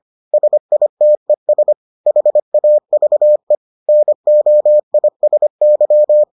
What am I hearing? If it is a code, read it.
Morse code, heard as a keyed tone, about 25 wpm, 600 Hz. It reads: SITES HAVE NOISY